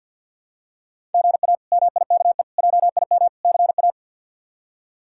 Morse code, heard as a keyed tone, about 51 wpm, 700 Hz. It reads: GA MIKE JIM CA
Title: K1USN SST QSO Practice - 50wpm